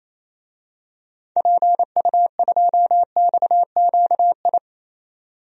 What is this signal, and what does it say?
Morse code, heard as a keyed tone, about 27 wpm, 700 Hz. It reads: PU2XQI
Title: DXCC-100 Call Signs - 6 Characters - 28wpm